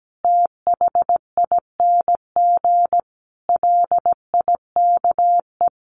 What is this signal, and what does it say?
Morse code, heard as a keyed tone, about 17 wpm, 700 Hz. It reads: THING LIKE
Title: A Sound Of Thunder 17wpm